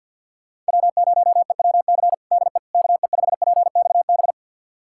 Morse code, heard as a keyed tone, about 47 wpm, 700 Hz. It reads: W0EWC DE KE5PXB
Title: USA Callsign DE Callsign 50wpm